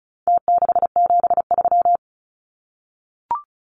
Morse code, heard as a keyed tone, about 34 wpm, 700 Hz. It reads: T673
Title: Numbers - Contest Style - 4-Digit 35wpm